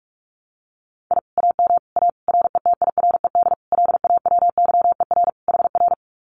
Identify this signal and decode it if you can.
Morse code, heard as a keyed tone, about 44 wpm, 700 Hz. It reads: I AM A RETIRED LAWYER HR
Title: Common QSO Phrases Encore 45wpm